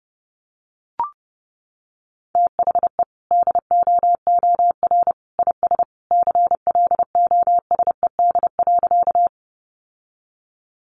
Morse code, heard as a keyed tone, about 30 wpm, 700 Hz. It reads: THE DOOR IS CLOSED.
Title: Sentences from Top 1400 Words 30wpm